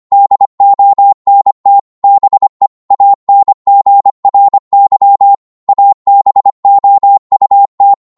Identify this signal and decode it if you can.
Morse code, heard as a keyed tone, about 25 wpm, 800 Hz. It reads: DONT BE ANGRY ABOUT